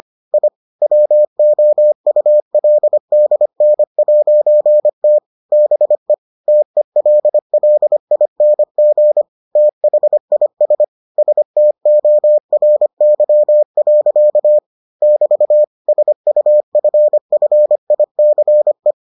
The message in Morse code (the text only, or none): I WOULDN'T BE TELLING THIS STORY. <BT> SUFFICE